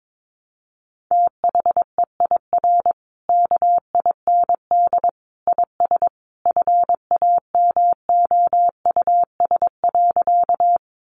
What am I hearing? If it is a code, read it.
Morse code, heard as a keyed tone, about 22 wpm, 700 Hz. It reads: THEIR KIND IS FAMOUS.
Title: Sentences from Top 1500 Words 22wpm